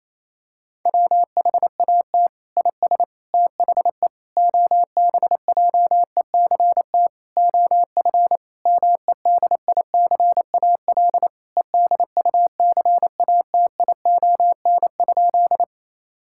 Morse code, heard as a keyed tone, about 28 wpm, 700 Hz. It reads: WHAT IS THE OBJECT OF MEDICAL EDUCATION?
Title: Sentences from Top 1000 Words - Review Part 6 - 28wpm